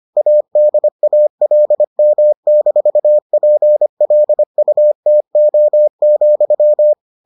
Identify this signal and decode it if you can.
Morse code, heard as a keyed tone, about 25 wpm, 600 Hz. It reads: ADALM-PLUTO,